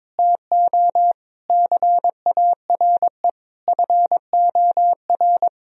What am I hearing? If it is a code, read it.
Morse code, heard as a keyed tone, about 21 wpm, 700 Hz. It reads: TO CARE FOR
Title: Sentences from Top 1000 Words - Review Part 3 - 22wpm